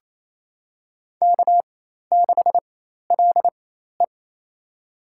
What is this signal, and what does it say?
Morse code, heard as a keyed tone, about 28 wpm, spaced as for 13 wpm, 700 Hz. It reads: K6LE
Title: USA Call Signs - 4 Characters - 4x Character Spacing - 28wpm